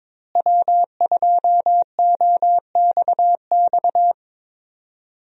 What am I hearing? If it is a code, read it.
Morse code, heard as a keyed tone, about 22 wpm, 700 Hz. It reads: W2OXX
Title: USA Call Signs - 5 Characters - Repeated 2x - 22wpm